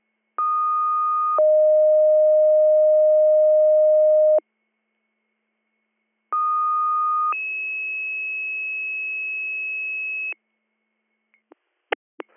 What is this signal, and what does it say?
Two-tone page, heard as a one-second tone, then a three-second tone, about 1230 then 620 Hz, about 1230 then 2600 Hz. It is heard from 0.4 to 10.3 s.